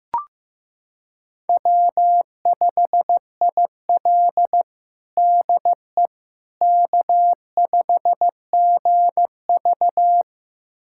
Morse code, heard as a keyed tone, about 15 wpm, 700 Hz. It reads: W5IL DE K5GV
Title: USA Callsign DE Callsign - 4 Characters - 15wpm